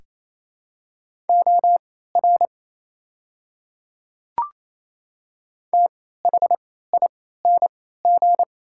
Morse code, heard as a keyed tone, about 28 wpm, spaced as for 16 wpm, 700 Hz. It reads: OR THING
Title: 3x Character Spacing - Top 100 Words - 28wpm